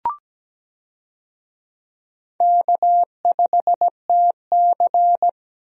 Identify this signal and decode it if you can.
Morse code, heard as a keyed tone, about 17 wpm, 700 Hz. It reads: K5TC
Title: USA Call Signs - 4 Characters - Repeated 2x - 17wpm